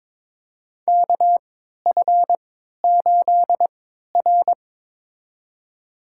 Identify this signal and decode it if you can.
Morse code, heard as a keyed tone, about 22 wpm, spaced as for 13 wpm, 690 Hz. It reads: KF8R
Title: USA Call Signs - 4 Characters - 3x Character Spacing - 22wpm